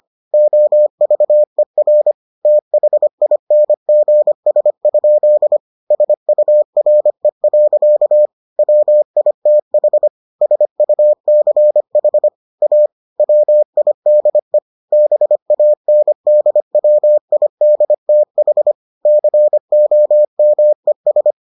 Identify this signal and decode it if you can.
Morse code, heard as a keyed tone, about 25 wpm, 600 Hz. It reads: OVER THINGS? SURE. WITH SUCH A WIDE BANDWIDTH COMES